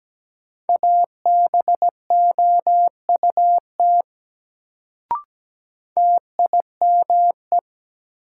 Morse code, heard as a keyed tone, about 17 wpm, 700 Hz. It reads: ABOUT TIME
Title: Sentences from Top 100 Words - New Word Drill - 17wpm